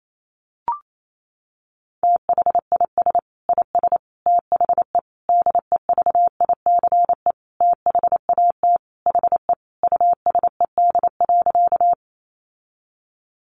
Morse code, heard as a keyed tone, about 28 wpm, 700 Hz. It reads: THIS IS THE DEVICE THAT HE USED.
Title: Sentences from Top 1500 Words 28wpm